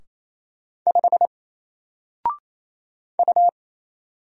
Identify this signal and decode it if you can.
Morse code, heard as a keyed tone, about 27 wpm, 700 Hz. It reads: 5 U
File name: Single-Letter-Number-28wpm